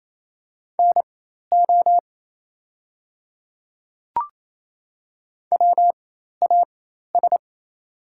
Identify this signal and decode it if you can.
Morse code, heard as a keyed tone, about 28 wpm, spaced as for 13 wpm, 690 Hz. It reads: NO WAS
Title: 4x Character Spacing - Top 100 Words - 28wpm